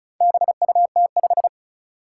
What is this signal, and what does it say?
Morse code, heard as a keyed tone, about 36 wpm, 700 Hz. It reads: BUT5
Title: Four Challenging Letter-Number Combinations 35wpm